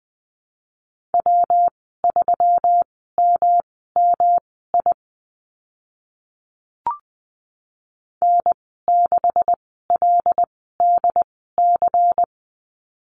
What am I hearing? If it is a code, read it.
Morse code, heard as a keyed tone, about 20 wpm, spaced as for 15 wpm, 700 Hz. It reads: W3MMI N6LDC
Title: 179 Lesson 29g - Slash - Call Signs 20wpm